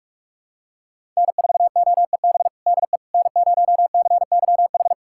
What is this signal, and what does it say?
Morse code, heard as a keyed tone, about 46 wpm, 700 Hz. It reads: N4OEB DE N0CYH
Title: USA Callsign DE Callsign 45wpm